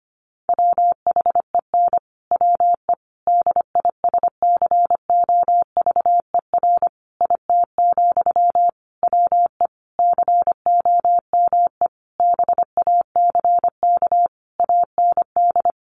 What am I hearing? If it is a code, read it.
Morse code, heard as a keyed tone, about 25 wpm, 700 Hz. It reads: WHEN WE DISCOVER IT, WE COME BACK AND
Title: Sentences from Top 1100 Words 25wpm